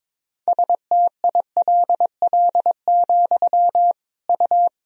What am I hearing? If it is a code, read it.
Morse code, heard as a keyed tone, about 22 wpm, 700 Hz. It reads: STILL, U
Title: Sentences from Top 300 Words and Abbreviations 22wpm